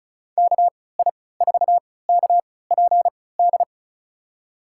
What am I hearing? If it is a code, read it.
Morse code, heard as a keyed tone, about 34 wpm, spaced as for 20 wpm, 700 Hz. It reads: KI4KPD
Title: USA Call Signs - 6 Characters - 3x Character Spacing - 35wpm